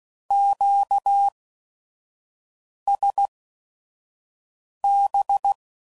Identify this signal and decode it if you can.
Morse code, heard as a keyed tone, about 16 wpm, spaced as for 5 wpm, 800 Hz. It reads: QSB